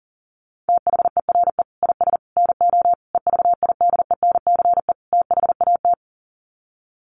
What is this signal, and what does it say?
Morse code, heard as a keyed tone, about 40 wpm, 700 Hz. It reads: THERE IS NO EVIDENCE THAT
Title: Sets of 5 Words Encore 40wpm